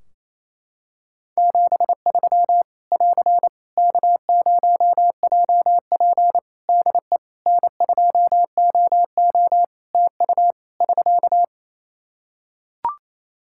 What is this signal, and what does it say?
Morse code, heard as a keyed tone, about 28 wpm, 700 Hz. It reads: 73 <AR> K0JP DE N2OO TU <SK>
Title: Practice QSO - Exchange 4 - Ending - 28wpm